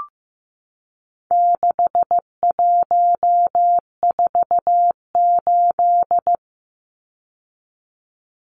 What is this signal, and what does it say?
Morse code, heard as a keyed tone, about 15 wpm, 700 Hz. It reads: 6148
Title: Numbers - 4-Digit 15wpm